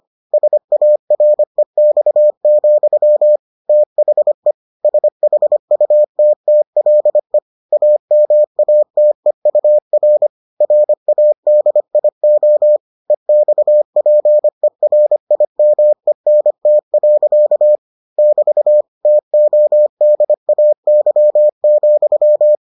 Morse code, heard as a keyed tone, about 25 wpm, 600 Hz. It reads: SAREX, THE SHUTTLE AMATEUR RADIO EXPERIMENT. <BT> TODAY,